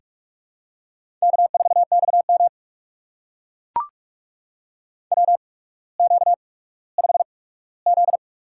Morse code, heard as a keyed tone, about 44 wpm, spaced as for 19 wpm, 700 Hz. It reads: K4XM WQ5Z
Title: USA Call Signs - 4 Characters - 8x Character Spacing - 45wpm